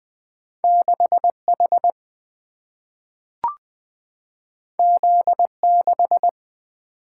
Morse code, heard as a keyed tone, about 20 wpm, 700 Hz. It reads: 6 H Z 6
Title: Single-Letter Single-Number 20wpm